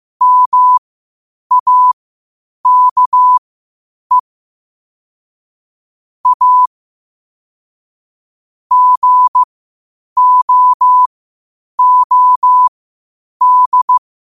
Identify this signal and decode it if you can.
Morse code, heard as a keyed tone, about 15 wpm, spaced as for 8 wpm, 1000 Hz. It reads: MAKE A GOOD
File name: qotdfiles / 8wpm